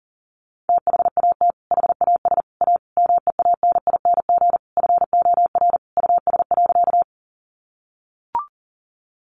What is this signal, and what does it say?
Morse code, heard as a keyed tone, about 40 wpm, 700 Hz. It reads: THAT HAS A MEANING FOR US.
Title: Sentences from Top 900 Words 40wpm